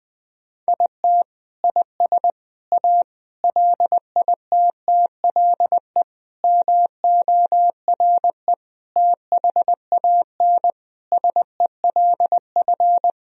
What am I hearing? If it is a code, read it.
Morse code, heard as a keyed tone, about 20 wpm, 700 Hz. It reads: IT IS A LITTLE MORE THAN SELF
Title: Sentences from Top 1000 Words - Review Part 1 - 20wpm